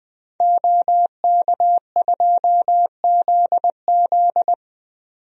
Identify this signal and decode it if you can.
Morse code, heard as a keyed tone, about 20 wpm, 700 Hz. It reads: OK2ZZ
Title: DXCC-100 Call Signs - 5 Characters - 20wpm